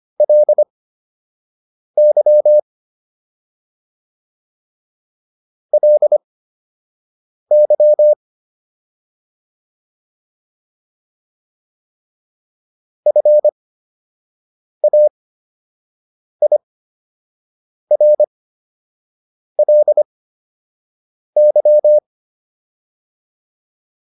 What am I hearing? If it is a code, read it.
Morse code, heard as a keyed tone, about 25 wpm, spaced as for 6 wpm, 600 Hz. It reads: LY LY FAIRLY